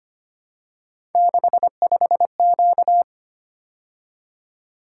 Morse code, heard as a keyed tone, about 25 wpm, 700 Hz. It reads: 65Q
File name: Three Letter-Number Combinations - 25wpm